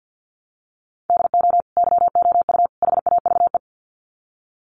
Morse code, heard as a keyed tone, about 50 wpm, 700 Hz. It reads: DO YOU HAVE
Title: Sets of 3 Words 50wpm